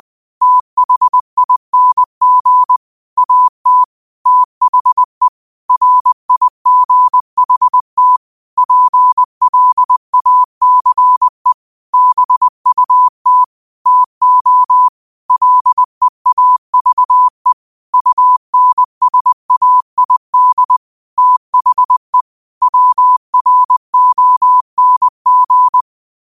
Morse code, heard as a keyed tone, about 20 wpm, 1000 Hz. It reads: THING AT THE RIGHT PLACE BUT TO LEAVE UNSAID THE WRONG